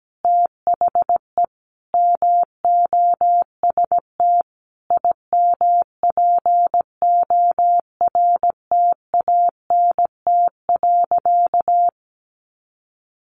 Morse code, heard as a keyed tone, about 17 wpm, 700 Hz. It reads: THE MOST IMPORTANT.
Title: Sentences from Top 700 Words 17wpm